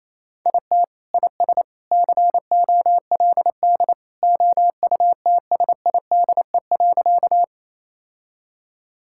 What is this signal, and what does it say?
Morse code, heard as a keyed tone, about 28 wpm, 700 Hz. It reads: IT IS COLD OUTSIDE.
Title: Sentences from Top 1000 Words - Review Part 6 - 28wpm